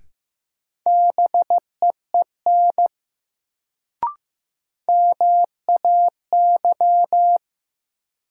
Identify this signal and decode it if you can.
Morse code, heard as a keyed tone, about 15 wpm, 700 Hz. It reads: BEEN MAY
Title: Sets of 2 Words - New Word Drill - 15wpm